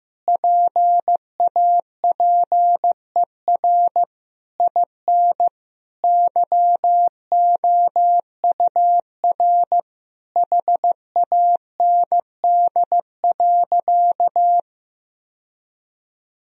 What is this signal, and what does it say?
Morse code, heard as a keyed tone, about 15 wpm, 700 Hz. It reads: PAPER IN YOUR HAND.
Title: Sentences from Top 1000 Words - Review Part 2 - 15wpm